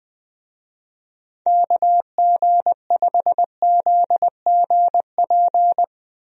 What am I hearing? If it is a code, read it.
Morse code, heard as a keyed tone, about 21 wpm, 700 Hz. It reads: KG5ZGP
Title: USA Call Signs - 6 Characters - Repeated 3x - 20wpm